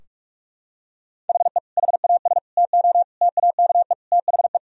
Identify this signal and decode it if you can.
Morse code, heard as a keyed tone, about 46 wpm, 700 Hz. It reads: HE HAS TO TAKE THE
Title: Sentences from Top 1000 Words - Review Part 2 - 45wpm